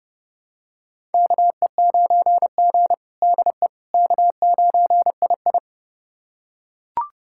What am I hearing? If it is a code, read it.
Morse code, heard as a keyed tone, about 29 wpm, 700 Hz. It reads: KE9G DE K9II
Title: USA Callsign DE Callsign - 4 Characters - 30wpm